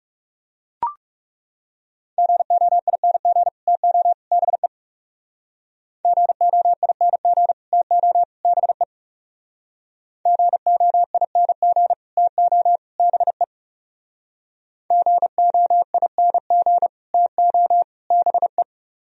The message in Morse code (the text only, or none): GOING TO BE GOING TO BE GOING TO BE GOING TO BE